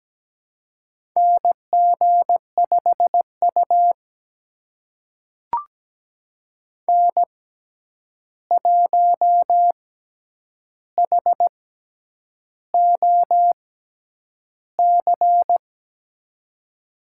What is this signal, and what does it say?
Morse code, heard as a keyed tone, about 17 wpm, spaced as for 8 wpm, 700 Hz. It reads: NG5U N1HOC
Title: USA Call Signs - 6x Character Spacing - 17wpm